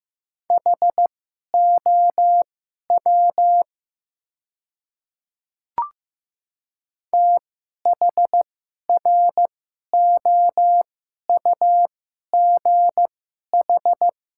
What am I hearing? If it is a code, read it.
Morse code, heard as a keyed tone, about 16 wpm, spaced as for 11 wpm, 700 Hz. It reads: HOW THROUGH
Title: 2x Character Spacing - Top 200 Words - 15wpm